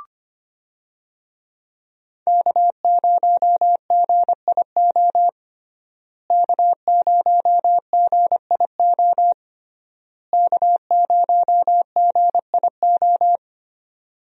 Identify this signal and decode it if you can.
Morse code, heard as a keyed tone, about 25 wpm, 700 Hz. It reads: K0GIO K0GIO K0GIO
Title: USA Call Signs - 5 Characters - Repeated 3x - 25wpm